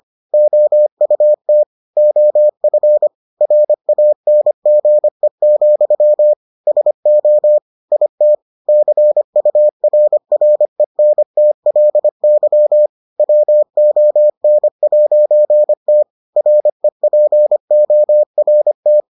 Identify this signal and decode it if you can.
Morse code, heard as a keyed tone, about 25 wpm, 600 Hz. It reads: OUT OF RANGE, SO IT CURRENTLY WON'T REPORT